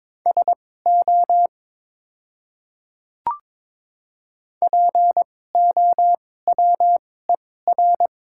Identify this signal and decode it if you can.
Morse code, heard as a keyed tone, about 22 wpm, spaced as for 16 wpm, 700 Hz. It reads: SO POWER
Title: 2x Character Spacing - Top 500 Words - 22wpm